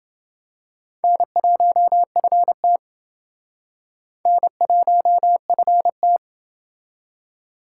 Morse code, heard as a keyed tone, about 28 wpm, 700 Hz. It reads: N1FT N1FT